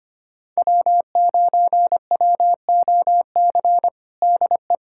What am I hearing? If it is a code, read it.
Morse code, heard as a keyed tone, about 25 wpm, 700 Hz. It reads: W9WOC DE
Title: USA Callsign DE Callsign - 5 Characters - 25wpm